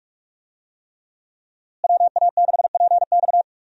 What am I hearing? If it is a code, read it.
Morse code, heard as a keyed tone, about 46 wpm, 700 Hz. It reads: WA6PX
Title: USA Call Signs - 5 Characters - Repeated 3x - 45wpm